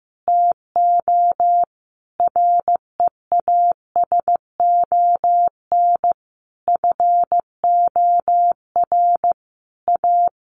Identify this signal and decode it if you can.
Morse code, heard as a keyed tone, about 15 wpm, 700 Hz. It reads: TO REASON FOR A
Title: Sentences from Top 700 Words 15wpm